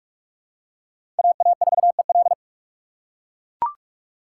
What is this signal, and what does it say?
Morse code, heard as a keyed tone, about 44 wpm, 700 Hz. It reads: AA4EL
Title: USA Call Signs - 5 Characters - 6x Character Spacing - 45wpm